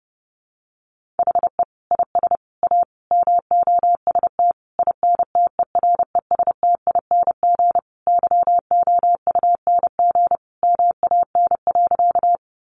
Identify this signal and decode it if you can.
Morse code, heard as a keyed tone, about 30 wpm, 700 Hz. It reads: HE IS A MOST INTERESTING YOUNG MAN.
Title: Sentences from Top 1000 Words 30wpm